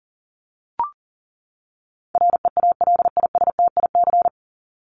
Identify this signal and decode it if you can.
Morse code, heard as a keyed tone, about 40 wpm, 700 Hz. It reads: REALISTIC
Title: Top 1751-2000 Words 40wpm